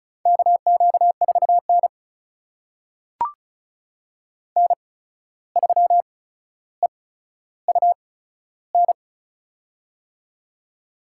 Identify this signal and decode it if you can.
Morse code, heard as a keyed tone, about 36 wpm, spaced as for 14 wpm, 700 Hz. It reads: KQ4N N3EUN
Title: USA Call Signs - 8x Character Spacing - 35wpm